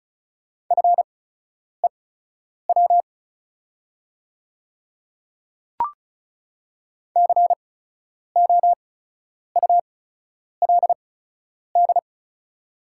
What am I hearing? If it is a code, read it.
Morse code, heard as a keyed tone, about 36 wpm, spaced as for 10 wpm, 700 Hz. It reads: FEW COULD
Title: 8x Character Spacing - Top 300 Words - 35wpm